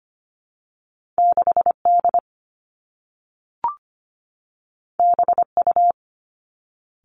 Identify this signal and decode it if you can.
Morse code, heard as a keyed tone, about 25 wpm, 700 Hz. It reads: 6 D B U
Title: Two Challenging Letter-Number Combinations 25wpm